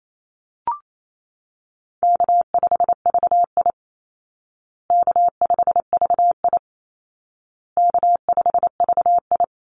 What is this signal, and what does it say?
Morse code, heard as a keyed tone, about 28 wpm, 700 Hz. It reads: K5VI K5VI K5VI
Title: Call Signs Repeated 2x - 28wpm